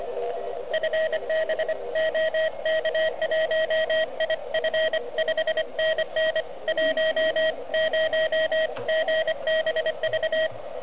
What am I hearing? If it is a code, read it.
Morse code, heard as a keyed tone, about 24 wpm, 620 Hz, with radio noise. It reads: FB OK1IF 5NN 10GBV
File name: ARCHIV / OK QRP 2021 / REC05